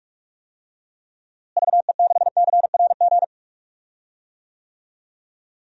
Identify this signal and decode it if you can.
Morse code, heard as a keyed tone, about 46 wpm, 700 Hz. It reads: VE6CRG
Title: DXCC-100 Call Signs - 6 Characters - Repeated 3x - 45wpm